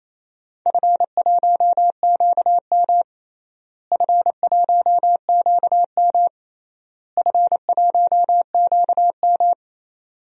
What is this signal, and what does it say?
Morse code, heard as a keyed tone, about 27 wpm, 700 Hz. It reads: F1QM F1QM F1QM
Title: DXCC-100 Call Signs - 4 Characters - Repeated 3x - 28wpm